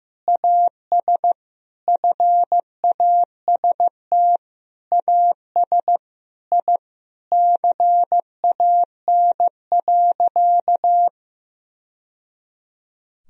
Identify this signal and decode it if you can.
Morse code, heard as a keyed tone, about 16 wpm, 700 Hz. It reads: AS FAST AS I CAN.
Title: Sentences from Top 1000 Words - Review Part 4 - 15wpm